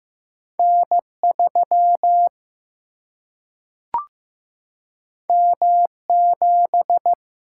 Morse code, heard as a keyed tone, about 15 wpm, 700 Hz. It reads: N3 M7
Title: Single-Letter Single-Number 15wpm